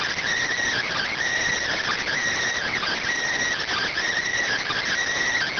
Emergency Alert System (SAME) tones, heard as harsh buzzing bursts. It is heard from 0.8 to 4.9 s.